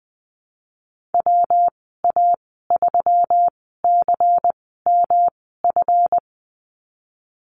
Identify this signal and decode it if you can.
Morse code, heard as a keyed tone, about 20 wpm, spaced as for 15 wpm, 700 Hz. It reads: WA3CMF
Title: 173 Lesson 28g - Number 9 - Call Signs 20wpm